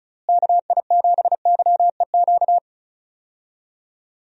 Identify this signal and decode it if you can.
Morse code, heard as a keyed tone, about 34 wpm, 700 Hz. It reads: KI7YEQ
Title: USA Call Signs - 6 Characters - Repeated 3x - 35wpm